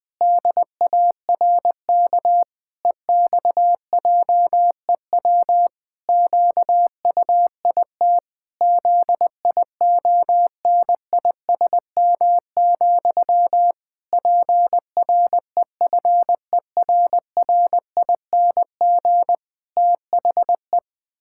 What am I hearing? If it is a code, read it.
Morse code, heard as a keyed tone, about 20 wpm, 700 Hz. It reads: DARK EXJEW QUIT ZIONISM, PREFERRING THE
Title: Pangram Challenge v2 - 20wpm